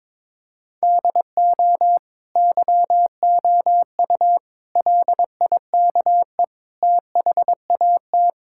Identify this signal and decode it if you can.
Morse code, heard as a keyed tone, about 22 wpm, 700 Hz. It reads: DO YOU LIKE THAT
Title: Sentences from Top 1000 Words - Review Part 5 - 22wpm